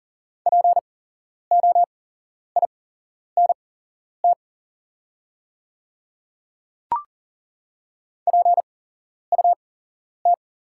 Morse code, heard as a keyed tone, about 40 wpm, spaced as for 11 wpm, 690 Hz. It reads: POINT PUT
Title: 8x Character Spacing - Top 200 Words - 40wpm